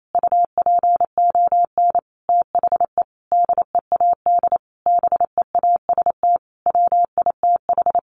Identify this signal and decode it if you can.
Morse code, heard as a keyed tone, about 28 wpm, 700 Hz. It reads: UPON THE DEAD BEAST WITH
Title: A Sound Of Thunder 28wpm